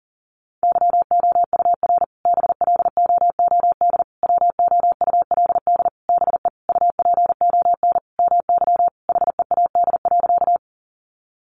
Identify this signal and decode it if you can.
Morse code, heard as a keyed tone, about 40 wpm, 700 Hz. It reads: YOUR BLOOD WOULD BE UPON MY HEAD.
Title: Sentences from Top 1100 Words 40wpm